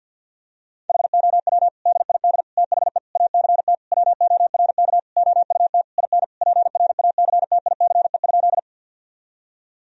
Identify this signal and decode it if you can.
Morse code, heard as a keyed tone, about 50 wpm, 700 Hz. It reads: HOW DID THE ACT WORK OUT IN PRACTICE?
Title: Sentences from Top 1000 Words - Review Part 4 - 50wpm